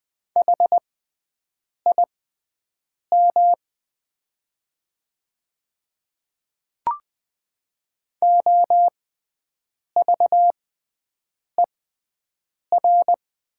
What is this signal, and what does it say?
Morse code, heard as a keyed tone, about 20 wpm, spaced as for 7 wpm, 700 Hz. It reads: HIM OVER
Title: 6x Character Spacing - Top 100 Words - 20wpm